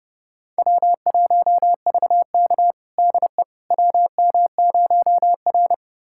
Morse code, heard as a keyed tone, about 31 wpm, 700 Hz. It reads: W1VK DE WM0R
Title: USA Callsign DE Callsign - 4 Characters - 30wpm